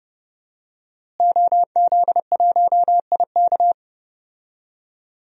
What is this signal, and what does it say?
Morse code, heard as a keyed tone, about 30 wpm, 700 Hz. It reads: OZ1IK
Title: DXCC-100 Call Signs - 5 Characters - Repeated 2x - 30wpm